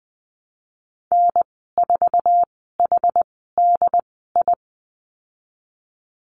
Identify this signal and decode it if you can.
Morse code, heard as a keyed tone, about 19 wpm, spaced as for 14 wpm, 700 Hz. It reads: N4HDI
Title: 179 Lesson 29g - Slash - Call Signs 20wpm